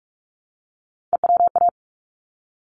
Morse code, heard as a keyed tone, about 44 wpm, 700 Hz. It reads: EWA
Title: ARRL Sections 45wpm